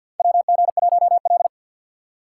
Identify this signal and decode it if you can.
Morse code, heard as a keyed tone, about 46 wpm, 700 Hz. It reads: WG1L
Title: USA Call Signs - 4 Characters - 2x Character Spacing - 50wpm